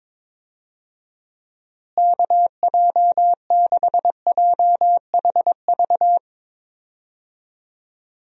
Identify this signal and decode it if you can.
Morse code, heard as a keyed tone, about 22 wpm, 700 Hz. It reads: KJ6JHV
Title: USA Call Signs - 6 Characters - Repeated 2x - 22wpm